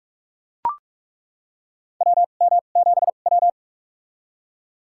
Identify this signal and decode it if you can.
Morse code, heard as a keyed tone, about 42 wpm, spaced as for 31 wpm, 700 Hz. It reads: WM7W
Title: USA Call Signs - 4 Characters - 2x Character Spacing - 45wpm